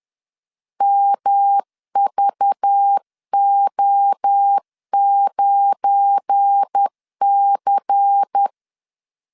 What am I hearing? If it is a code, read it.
Morse code, heard as a keyed tone, about 10 wpm, 800 Hz. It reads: MVO9C
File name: LEZIONI / 7 GF9Z / 7 GF9 10wpm 2747